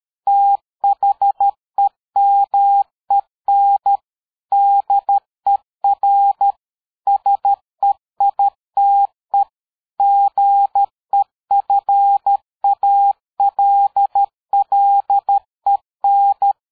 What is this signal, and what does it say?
Morse code, heard as a keyed tone, about 13 wpm, 800 Hz. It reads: THEMEN DER SEITE GEFALLEN
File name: dk5ke / PersWorte 60